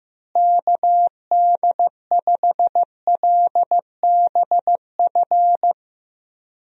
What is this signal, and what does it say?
Morse code, heard as a keyed tone, about 15 wpm, 700 Hz. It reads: KD5LBF
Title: USA Call Signs - 6 Characters - Repeated 3x - 15wpm